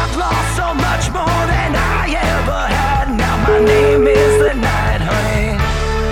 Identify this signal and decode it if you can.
Ringback tone (Europe, 425 Hz), with noise.